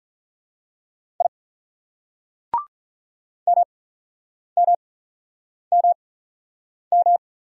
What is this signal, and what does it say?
Morse code, heard as a keyed tone, about 45 wpm, spaced as for 9 wpm, 700 Hz. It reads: I MMMM